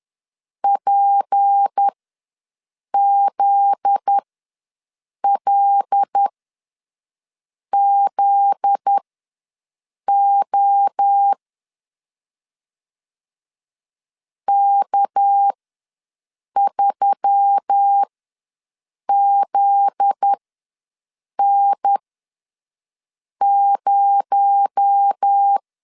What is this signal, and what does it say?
Morse code, heard as a keyed tone, about 10 wpm, spaced as for 6 wpm, 800 Hz. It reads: PZLZO K3ZN0
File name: LEZIONI / 7 GF9Z / 7 GF9Z ws 10wpm 3919